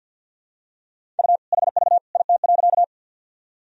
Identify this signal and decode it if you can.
Morse code, heard as a keyed tone, about 46 wpm, 700 Hz. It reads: U HV IT.
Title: Sentences from Top 100 Words and Abbreviations 50wpm